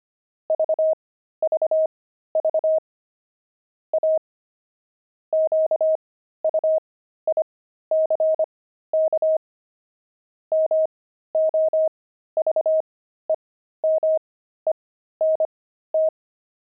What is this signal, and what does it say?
Morse code, heard as a keyed tone, about 25 wpm, spaced as for 13 wpm, 630 Hz. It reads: VVV A QUICK MOVEMENT
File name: pangram-fs-13-cs-25-enemy